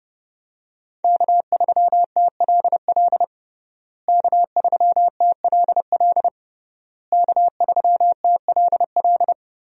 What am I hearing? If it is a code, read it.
Morse code, heard as a keyed tone, about 30 wpm, 700 Hz. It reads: K3TLL K3TLL K3TLL
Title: USA Call Signs - 5 Characters - Repeated 3x - 30wpm